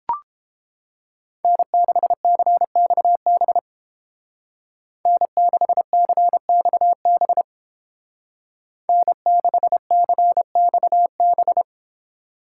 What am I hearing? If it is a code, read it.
Morse code, heard as a keyed tone, about 30 wpm, 700 Hz. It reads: N6CXB N6CXB N6CXB